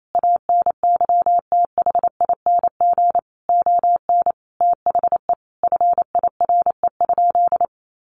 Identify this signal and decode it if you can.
Morse code, heard as a keyed tone, about 28 wpm, 700 Hz. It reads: ANYTHING ON THE FIRE?
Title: Sentences from Top 1000 Words 28wpm